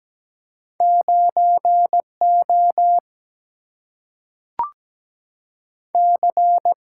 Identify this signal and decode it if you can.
Morse code, heard as a keyed tone, about 17 wpm, 700 Hz. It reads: 9 O C
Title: Single-Letter Single-Number 17wpm